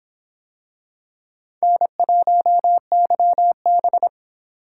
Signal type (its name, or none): Morse code